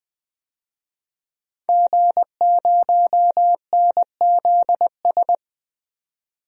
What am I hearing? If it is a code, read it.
Morse code, heard as a keyed tone, about 20 wpm, 700 Hz. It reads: G0NZS
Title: DXCC-100 Call Signs - 5 Characters - 20wpm